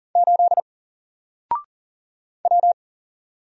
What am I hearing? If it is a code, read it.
Morse code, heard as a keyed tone, about 39 wpm, 700 Hz. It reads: 8 W
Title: Single Character (Letter:Number:Punctuation:Prosign) 40wpm